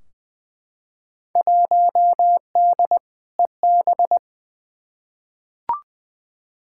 Morse code, heard as a keyed tone, about 20 wpm, 700 Hz. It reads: 1D EB
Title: Field Day Exchanges - 20wpm